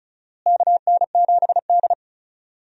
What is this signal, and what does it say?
Morse code, heard as a keyed tone, about 36 wpm, 700 Hz. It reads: KN7D
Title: USA Call Signs - 4 Characters - Repeated 2x - 35wpm